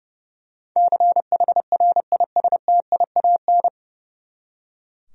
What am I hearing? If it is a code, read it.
Morse code, heard as a keyed tone, about 30 wpm, 700 Hz. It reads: CHRISTIAN
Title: USA Common Names - 30wpm